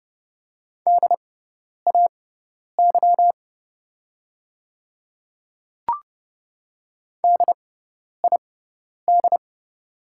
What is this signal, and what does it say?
Morse code, heard as a keyed tone, about 30 wpm, spaced as for 10 wpm, 690 Hz. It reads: DAY DID
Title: 6x Character Spacing - Top 200 Words - 30wpm